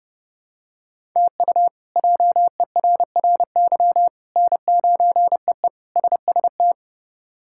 Morse code, heard as a keyed tone, about 32 wpm, 700 Hz. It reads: TU JERRY N9EE SST
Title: K1USN SST QSO Practice - 30wpm